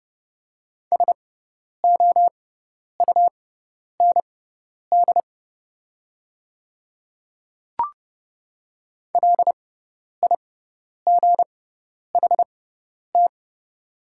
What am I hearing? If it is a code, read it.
Morse code, heard as a keyed tone, about 30 wpm, spaced as for 10 wpm, 700 Hz. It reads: SOUND LIGHT
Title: 6x Character Spacing - Top 300 Words - 30wpm